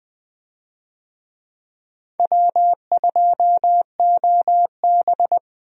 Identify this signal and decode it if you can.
Morse code, heard as a keyed tone, about 20 wpm, 700 Hz. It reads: W2OB
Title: USA Call Signs - 4 Characters - 20wpm